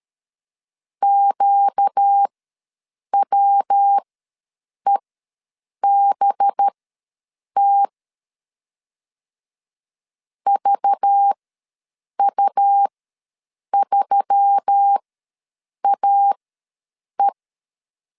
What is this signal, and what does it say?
Morse code, heard as a keyed tone, about 13 wpm, spaced as for 7 wpm, 800 Hz. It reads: QWEBT VU3AE
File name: LEZIONI / 5 LBVC / 5 LBV ws 12wpm 3015